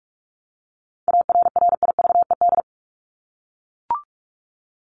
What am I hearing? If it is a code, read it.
Morse code, heard as a keyed tone, about 44 wpm, 700 Hz. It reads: ARRIVED
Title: ED Suffix - Top 100 Words with Base Words 45wpm